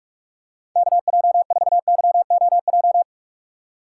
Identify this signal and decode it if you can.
Morse code, heard as a keyed tone, about 46 wpm, 700 Hz. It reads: KJ4YOJ
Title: USA Call Signs - 6 Characters - 2x Character Spacing - 45wpm